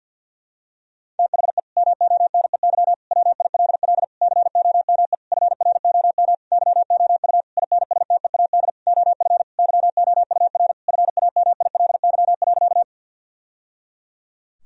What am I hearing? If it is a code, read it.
Morse code, heard as a keyed tone, about 50 wpm, 700 Hz. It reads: THE MONEY WILL COME FROM YOU INSTEAD OF YOUR FAMILY.
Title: Sentences from Top 1000 Words - Review Part 4 - 50wpm